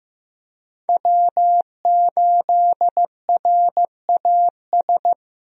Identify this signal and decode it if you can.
Morse code, heard as a keyed tone, about 15 wpm, 700 Hz. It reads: W8RAS
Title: USA Call Signs - 5 Characters - 15wpm